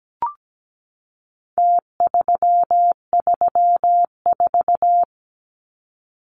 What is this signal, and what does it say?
Morse code, heard as a keyed tone, about 17 wpm, 700 Hz. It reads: T334
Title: Numbers - Contest Style - 4-Digit 17wpm